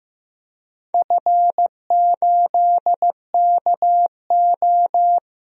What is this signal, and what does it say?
Morse code, heard as a keyed tone, about 15 wpm, 700 Hz. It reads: F8KO
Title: DXCC-100 Call Signs - 4 Characters - 15wpm